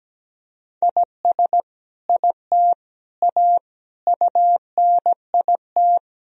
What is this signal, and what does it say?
Morse code, heard as a keyed tone, about 17 wpm, 700 Hz. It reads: IS IT A UNIT
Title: Sentences from Top 1000 Words - Review Part 2 - 17wpm